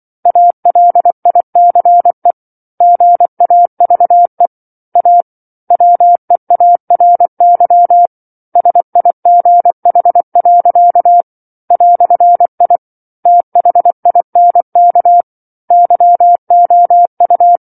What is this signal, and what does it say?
Morse code, heard as a keyed tone, about 24 wpm, 700 Hz. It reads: ALICE GAVE A WEARY SIGH. "I THINK YOU